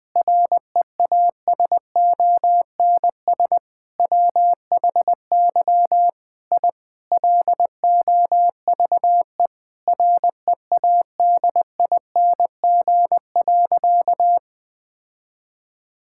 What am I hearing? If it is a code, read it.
Morse code, heard as a keyed tone, about 20 wpm, 700 Hz. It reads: REASONS WHY I LOVE READING.
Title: Sentences with 7-Word Phrases - Spoken Beginning - 20wpm